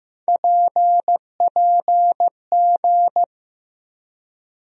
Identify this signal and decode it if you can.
Morse code, heard as a keyed tone, about 15 wpm, 700 Hz. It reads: PPG
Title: USA Call Sign Postfixes - 3 Letters 15wpm